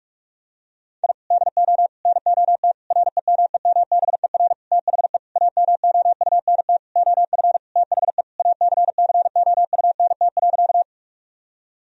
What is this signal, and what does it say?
Morse code, heard as a keyed tone, about 41 wpm, 700 Hz. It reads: I DO NOT REMEMBER THE AMOUNT OF THE ACCOUNT.
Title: Sentences from Top 1000 Words - Review Part 4 - 45wpm